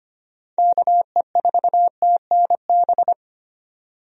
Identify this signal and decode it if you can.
Morse code, heard as a keyed tone, about 25 wpm, 700 Hz. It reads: KE4TNB
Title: USA Call Signs - 6 Characters - 25wpm